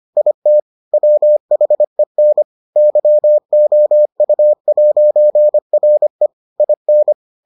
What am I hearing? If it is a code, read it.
Morse code, heard as a keyed tone, about 25 wpm, 600 Hz. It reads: IT WHEN YOU'RE IN